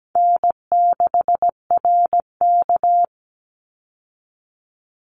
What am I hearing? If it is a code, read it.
Morse code, heard as a keyed tone, about 17 wpm, 700 Hz. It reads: N6RK
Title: Call Signs - 17wpm